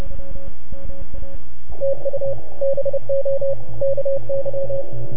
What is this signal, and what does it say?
Morse code, heard as a keyed tone, about 28 wpm, 530 Hz. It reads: TU BO KY
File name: NAQPCW-8-2009 / W1UJ NAQP CW 09 0100Z